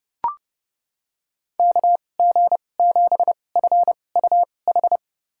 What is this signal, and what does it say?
Morse code, heard as a keyed tone, about 30 wpm, spaced as for 22 wpm, 700 Hz. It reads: KG7FUH
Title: USA Call Signs - 2x Character Spacing - 30wpm